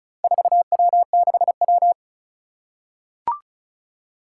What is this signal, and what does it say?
Morse code, heard as a keyed tone, about 36 wpm, 700 Hz. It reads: 4W6W
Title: Four Challenging Letter-Number Combinations 35wpm